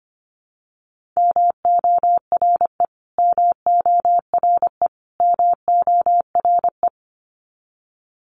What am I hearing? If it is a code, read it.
Morse code, heard as a keyed tone, about 25 wpm, 700 Hz. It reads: MORE MORE MORE
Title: Top 500 Words Repeated 25wpm